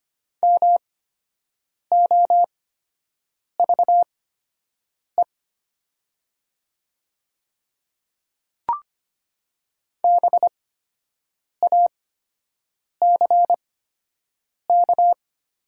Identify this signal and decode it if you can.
Morse code, heard as a keyed tone, about 25 wpm, spaced as for 7 wpm, 700 Hz. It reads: MOVE BACK
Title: 8x Character Spacing - Top 200 Words - 25wpm